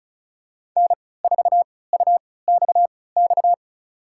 Morse code, heard as a keyed tone, about 34 wpm, spaced as for 20 wpm, 700 Hz. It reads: N4UXX
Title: USA Call Signs - 5 Characters - 3x Character Spacing - 35wpm